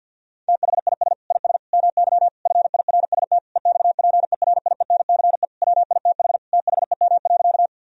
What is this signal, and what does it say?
Morse code, heard as a keyed tone, about 50 wpm, 700 Hz. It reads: THIS IS MY FIRST EXPERIENCE WITH THEM.
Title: Sentences from Top 1000 Words - Review Part 3 - 50wpm